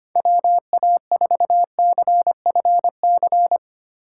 Morse code, heard as a keyed tone, about 25 wpm, 700 Hz. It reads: WA4CFC
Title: USA Call Signs - 6 Characters - 3x Character Spacing - 25wpm